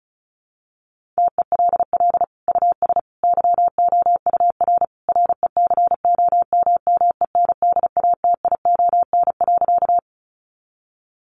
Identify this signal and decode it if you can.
Morse code, heard as a keyed tone, about 34 wpm, 700 Hz. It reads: TELL US YOUR RECOMMENDATION.
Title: Sentences from Top 1400 Words 35wpm